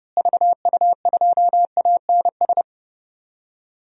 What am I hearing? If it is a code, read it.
Morse code, heard as a keyed tone, about 30 wpm, 700 Hz. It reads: VU2ANS
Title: DXCC-100 Call Signs - 6 Characters - Repeated 3x - 30wpm